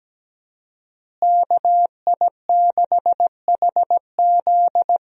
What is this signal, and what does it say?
Morse code, heard as a keyed tone, about 17 wpm, 700 Hz. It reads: KI6HZ
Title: USA Call Signs - 5 Characters - Repeated 3x - 17wpm